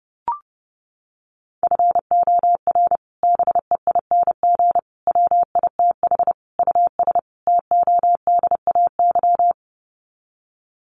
Morse code, heard as a keyed tone, about 30 wpm, 700 Hz. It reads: FOR BEING WITH US TODAY
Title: Sets of 5 Words Encore2 30wpm